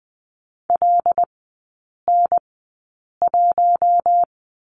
Morse code, heard as a keyed tone, about 20 wpm, spaced as for 8 wpm, 700 Hz. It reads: LN1
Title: 166 Lesson 27e - Number 7 - Three Characters 20wpm